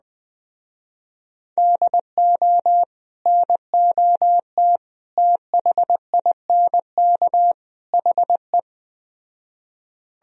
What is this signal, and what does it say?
Morse code, heard as a keyed tone, about 20 wpm, 700 Hz. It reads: DO NOT THINK HE